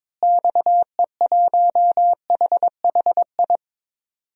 Morse code, heard as a keyed tone, about 22 wpm, 700 Hz. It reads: XE1HHI
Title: DXCC-100 Call Signs - 6 Characters - Repeated 3x - 22wpm